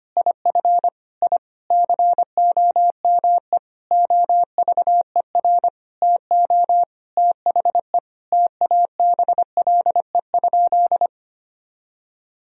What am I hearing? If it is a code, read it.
Morse code, heard as a keyed tone, about 25 wpm, 700 Hz. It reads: IF I COME OVER TO THE TABLE?
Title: Sentences from Top 1000 Words - Review Part 2 - 25wpm